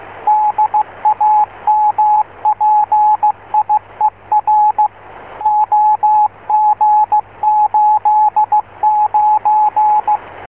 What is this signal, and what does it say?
Morse code, heard as a keyed tone, about 16 wpm, 850 Hz, with radio noise. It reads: DAMPIER OG89
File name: yt1vp / vk6rsx[1]